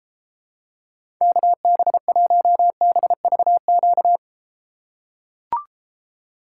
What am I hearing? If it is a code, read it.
Morse code, heard as a keyed tone, about 33 wpm, 700 Hz. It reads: KB1BVQ